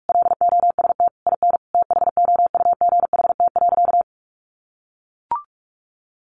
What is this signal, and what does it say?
Morse code, heard as a keyed tone, about 44 wpm, 700 Hz. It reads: LOST IN THOUGHT.
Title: Sentences from Top 1400 Words 45wpm